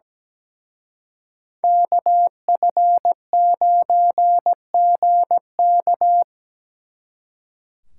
Morse code, heard as a keyed tone, about 20 wpm, 700 Hz. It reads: KF9GK
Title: Speed Racing - USA 2x2 Call Signs 17wpm